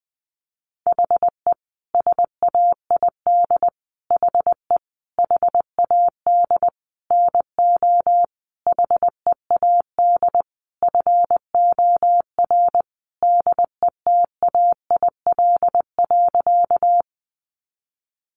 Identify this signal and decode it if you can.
Morse code, heard as a keyed tone, about 20 wpm, 700 Hz. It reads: HE SAID HE HAD NO HEAD FOR DETAIL.
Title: Sentences from Top 1500 Words 20wpm